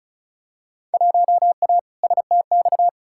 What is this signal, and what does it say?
Morse code, heard as a keyed tone, about 36 wpm, 700 Hz. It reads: 1A STX
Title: Field Day Exchanges Encore - 35wpm